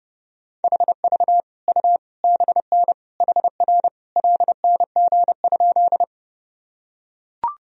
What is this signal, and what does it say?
Morse code, heard as a keyed tone, about 30 wpm, 700 Hz. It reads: HV U BN HR LNG?
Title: Sentences from Top 200 Words and Abbreviations 30wpm